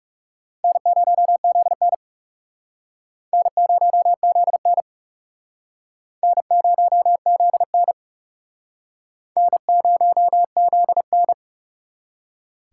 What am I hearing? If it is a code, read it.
Morse code, heard as a keyed tone, about 38 wpm, 700 Hz. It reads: N0ZN N0ZN N0ZN N0ZN